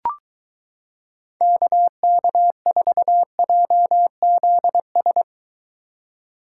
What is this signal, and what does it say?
Morse code, heard as a keyed tone, about 23 wpm, 700 Hz. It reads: KK4JZS